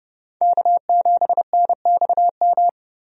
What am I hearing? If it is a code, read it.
Morse code, heard as a keyed tone, about 32 wpm, 700 Hz. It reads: K7NXM
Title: USA Call Signs - 5 Characters - Repeated 2x - 30wpm